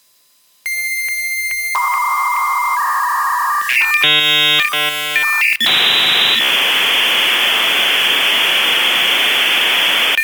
Dial-up modem handshake.